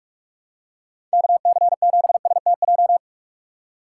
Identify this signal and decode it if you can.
Morse code, heard as a keyed tone, about 41 wpm, 700 Hz. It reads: KC7STJ
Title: USA Call Signs - 6 Characters - 2x Character Spacing - 45wpm